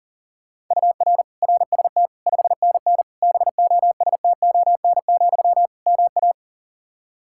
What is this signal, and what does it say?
Morse code, heard as a keyed tone, about 40 wpm, 700 Hz. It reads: UR RST 5NN BOSTON, MA
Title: Practice QSO - Exchange 1 - Beginning - 40wpm